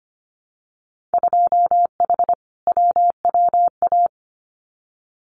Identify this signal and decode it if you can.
Morse code, heard as a keyed tone, about 25 wpm, 700 Hz. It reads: 2H WWA
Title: Winter Field Day Exchanges 25wpm